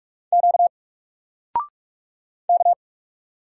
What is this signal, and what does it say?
Morse code, heard as a keyed tone, about 48 wpm, 700 Hz. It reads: Q K
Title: Single Letter-Number Alt 45wpm